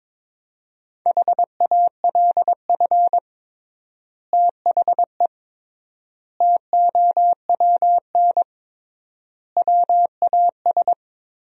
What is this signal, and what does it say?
Morse code, heard as a keyed tone, about 22 wpm, 700 Hz. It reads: HALF THE TOWN WAS
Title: 3x Word-Spacing - Sentences from Top 500 Words - 22wpm